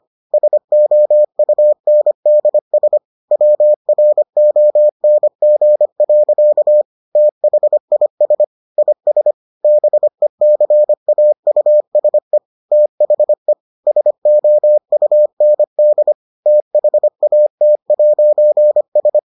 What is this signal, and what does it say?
Morse code, heard as a keyed tone, about 25 wpm, 600 Hz. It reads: SOUNDS WRONG. THIS IS BECAUSE THE SOUND THAT'S